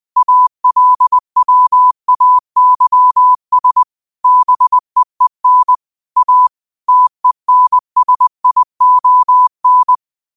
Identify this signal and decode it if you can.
Morse code, heard as a keyed tone, about 20 wpm, 1000 Hz. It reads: ALWAYS BEEN A TENSION